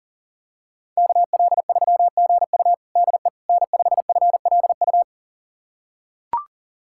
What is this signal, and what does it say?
Morse code, heard as a keyed tone, about 40 wpm, 700 Hz. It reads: KL3GU DE N5FLU
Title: USA Callsign DE Callsign - 5 Characters - 40wpm